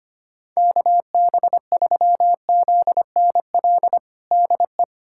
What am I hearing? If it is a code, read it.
Morse code, heard as a keyed tone, about 25 wpm, 700 Hz. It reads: KB3ZNL DE
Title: USA Callsign DE Callsign - 6 Characters - 25wpm